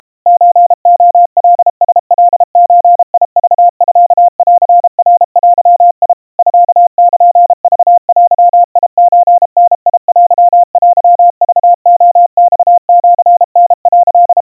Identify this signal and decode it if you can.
Morse code (Japanese Wabun code), heard as a keyed tone, about 32 wpm, 700 Hz. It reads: ソレカラカゾクミンナデ ミルクデソダテテクレマシタ」
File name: kawa / Morse sample / kamyu 120 cpm 700 Hz Crispy